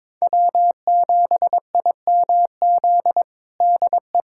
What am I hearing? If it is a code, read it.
Morse code, heard as a keyed tone, about 21 wpm, 700 Hz. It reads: W7IMZ DE
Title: USA Callsign DE Callsign - 5 Characters - 22wpm